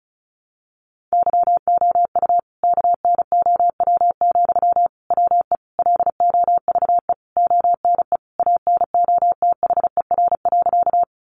Morse code, heard as a keyed tone, about 34 wpm, 700 Hz. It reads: YOU KNOW, WE LOVE ONE ANOTHER.
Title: Sentences from Top 600 Words 35wpm